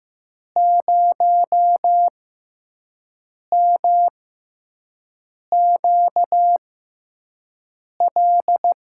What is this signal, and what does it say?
Morse code, heard as a keyed tone, about 15 wpm, spaced as for 5 wpm, 700 Hz. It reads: 0MQL
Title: USA Call Signs - 6 Characters - 6x Character Spacing - 15wpm